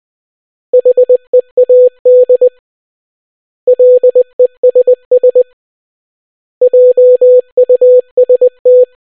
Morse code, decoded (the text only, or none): HEAD LESS JUST